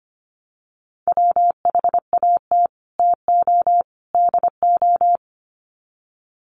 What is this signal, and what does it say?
Morse code, heard as a keyed tone, about 25 wpm, 700 Hz. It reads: WHAT TO DO
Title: Sets of 3 Words 25wpm